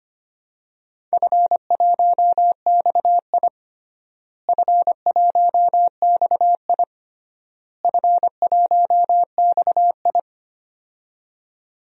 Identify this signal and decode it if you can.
Morse code, heard as a keyed tone, about 25 wpm, 700 Hz. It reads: F1XI F1XI F1XI
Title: DXCC-100 Call Signs - 4 Characters - Repeated 3x - 25wpm